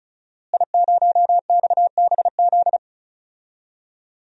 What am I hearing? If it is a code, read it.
Morse code, heard as a keyed tone, about 36 wpm, 700 Hz. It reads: I0XBZ
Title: DXCC-100 Call Signs - 5 Characters - Repeated 2x - 35wpm